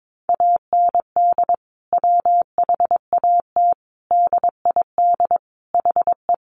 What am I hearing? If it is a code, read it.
Morse code, heard as a keyed tone, about 22 wpm, 700 Hz. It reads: AND WHAT DID HE
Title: Sentences from Top 700 Words 22wpm